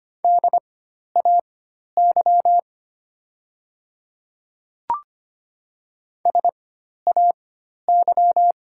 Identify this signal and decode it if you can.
Morse code, heard as a keyed tone, about 25 wpm, spaced as for 12 wpm, 700 Hz. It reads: DAY SAY
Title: 4x Character Spacing - Top 200 Words - 25wpm